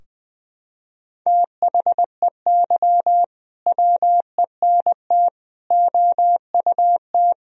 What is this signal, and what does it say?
Morse code, heard as a keyed tone, about 20 wpm, 700 Hz. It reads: THEY WENT OUT
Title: Sentences from Top 1000 Words - Review Part 2 - 20wpm